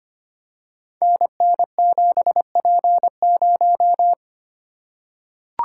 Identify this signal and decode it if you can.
Morse code, heard as a keyed tone, about 25 wpm, 700 Hz. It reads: NN7P0
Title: Code Groups - 5 Letter:Numbers - 25wpm